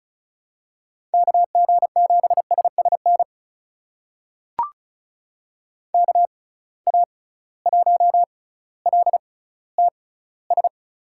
Morse code, heard as a keyed tone, about 36 wpm, spaced as for 18 wpm, 700 Hz. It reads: KG7SSN KA1LTS
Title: USA Call Signs - 6x Character Spacing - 35wpm